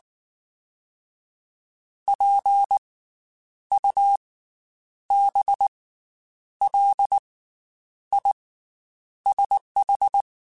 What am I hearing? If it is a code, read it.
Morse code, heard as a keyed tone, about 19 wpm, spaced as for 8 wpm, 800 Hz. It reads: PUBLISH
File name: Special INGs / Special INGs-mp3s / Slow ing @18